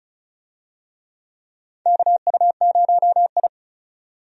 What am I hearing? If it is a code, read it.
Morse code, heard as a keyed tone, about 36 wpm, 700 Hz. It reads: KU0I
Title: USA Call Signs - 4 Characters - 35wpm